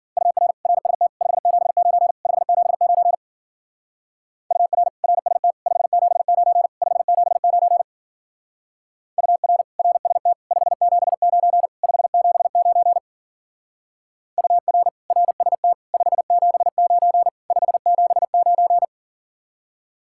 Morse code, heard as a keyed tone, about 48 wpm, 700 Hz. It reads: UR RST 579 579 UR RST 579 579 UR RST 579 579 UR RST 579 579
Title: Speed-Racing Common QSO Phrases Encore - 40wpm